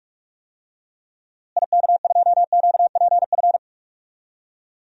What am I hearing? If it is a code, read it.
Morse code, heard as a keyed tone, about 44 wpm, 700 Hz. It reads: IK2QPF
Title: DXCC-100 Call Signs - 6 Characters - Repeated 2x - 45wpm